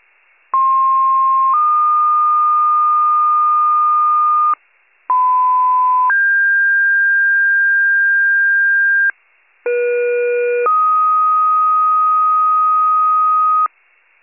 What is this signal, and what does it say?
Two-tone page, heard as a one-second tone, then a three-second tone, about 1060 then 1220 Hz, about 1010 then 1600 Hz, about 480 then 1250 Hz. It is heard from 0.5 to 13.7 s.